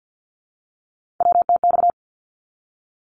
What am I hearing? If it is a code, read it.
Morse code, heard as a keyed tone, about 50 wpm, 700 Hz. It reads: WTX
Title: ARRL Sections 50wpm